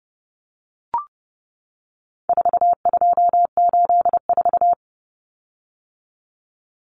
Morse code, heard as a keyed tone, about 30 wpm, 700 Hz. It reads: 4284
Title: Common Cut Numbers - 4-Digit 30wpm